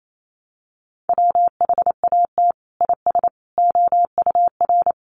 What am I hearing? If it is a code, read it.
Morse code, heard as a keyed tone, about 28 wpm, 700 Hz. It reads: WHAT IS OUR
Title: Sentences from Top 1200 Words 28wpm